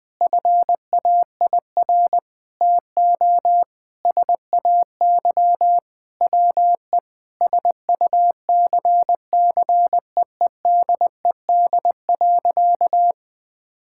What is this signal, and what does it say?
Morse code, heard as a keyed tone, about 20 wpm, 700 Hz. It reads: FAIR TO SAY WE SUCCEEDED.
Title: Sentences with 6-Word Phrases - Spoken Beginning - 20wpm